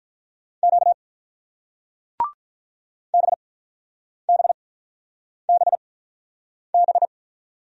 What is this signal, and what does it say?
Morse code, heard as a keyed tone, about 48 wpm, spaced as for 9 wpm, 700 Hz. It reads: Q BBBB